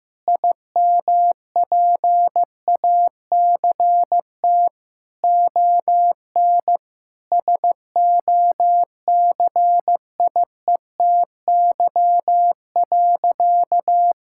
Morse code, heard as a keyed tone, about 15 wpm, 700 Hz. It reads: IMPACT ON SOCIETY.
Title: Sentences from Top 1000 Words - Review Part 6 - 15wpm